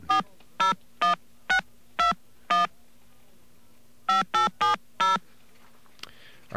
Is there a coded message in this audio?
Yes: DTMF from 0.1 to 5.2 s, heard as telephone touch tones, tones of about 140 ms. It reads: 401621 5870